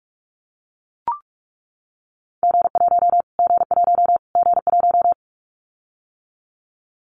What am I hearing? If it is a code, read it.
Morse code, heard as a keyed tone, about 44 wpm, 700 Hz. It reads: G1 G1 G1
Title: Callsign Prefixes for Top 25 DXCC - Repeated 3x - 45wpm